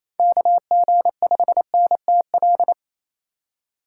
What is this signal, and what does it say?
Morse code, heard as a keyed tone, about 27 wpm, 700 Hz. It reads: KG5NTL
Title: USA Call Signs - 6 Characters - 2x Character Spacing - 28wpm